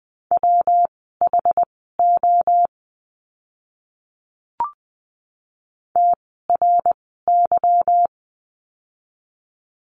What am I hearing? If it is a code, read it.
Morse code, heard as a keyed tone, about 20 wpm, spaced as for 14 wpm, 700 Hz. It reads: WHO TRY
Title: 244 Lesson 39f - Letter z - Words 20wpm